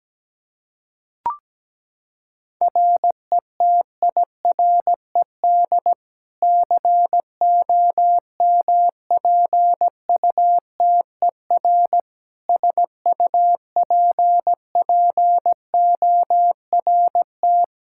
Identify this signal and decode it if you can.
Morse code, heard as a keyed tone, about 17 wpm, 700 Hz. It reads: RETIRED COMPUTER SUPPORT
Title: Practice QSO - Exchange 3 - More Info - 17wpm